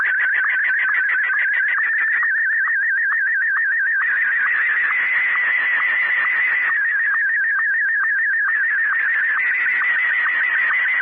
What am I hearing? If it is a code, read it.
SSTV (slow-scan television), Martin 1.
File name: Z PI4VRZ / PI4VRZ 14-jun-2025